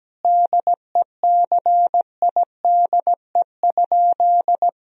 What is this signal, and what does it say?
Morse code, heard as a keyed tone, about 17 wpm, 700 Hz. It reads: DECIDE?
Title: Sentences from Top 1000 Words - Review Part 3 - 17wpm